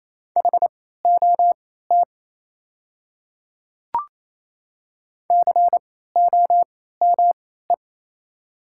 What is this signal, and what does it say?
Morse code, heard as a keyed tone, about 28 wpm, spaced as for 16 wpm, 700 Hz. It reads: HOT COME
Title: 3x Character Spacing - Top 100 Words - 28wpm